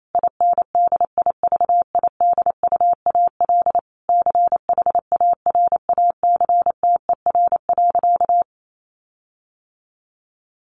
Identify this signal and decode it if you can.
Morse code, heard as a keyed tone, about 28 wpm, 700 Hz. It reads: INDIVIDUAL CHARACTER.
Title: Sentences from Top 1500 Words 28wpm